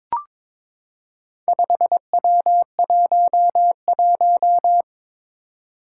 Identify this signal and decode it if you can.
Morse code, heard as a keyed tone, about 22 wpm, 700 Hz. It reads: 5W11
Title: Four Challenging Letter-Number Combinations 22wpm